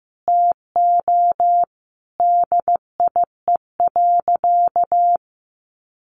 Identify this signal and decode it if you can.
Morse code, heard as a keyed tone, about 15 wpm, 700 Hz. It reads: TO DIE.
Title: Sentences from Top 1300 Words 15wpm